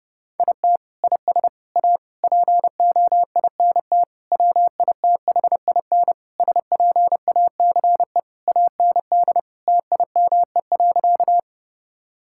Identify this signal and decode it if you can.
Morse code, heard as a keyed tone, about 32 wpm, 700 Hz. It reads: IT IS A POINT WITHIN SPACE AND TIME.
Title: Sentences from Top 1000 Words - Review Part 5 - 30wpm